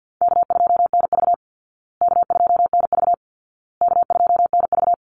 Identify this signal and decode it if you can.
Morse code, heard as a keyed tone, about 50 wpm, 700 Hz. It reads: K2NV K2NV K2NV
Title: Easy Call Signs Repeated 3x - 50wpm